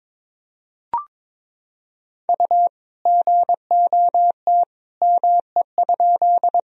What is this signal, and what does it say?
Morse code, heard as a keyed tone, about 22 wpm, 700 Hz. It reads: U GOT ME?
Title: Sentences from Top 300 Words and Abbreviations 22wpm